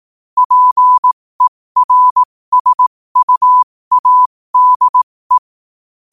Morse code, heard as a keyed tone, about 18 wpm, spaced as for 15 wpm, 1000 Hz. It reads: PERSUADE
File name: qotdfiles / 15wpm